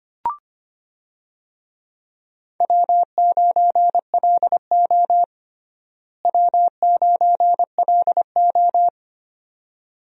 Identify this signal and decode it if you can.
Morse code, heard as a keyed tone, about 25 wpm, 700 Hz. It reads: W9LO W9LO
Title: USA Call Signs - 4 Characters - Repeated 2x - 25wpm